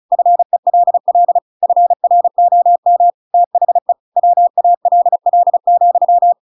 Morse code, heard as a keyed tone, about 34 wpm, 700 Hz. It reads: FELL FROM THE WALL,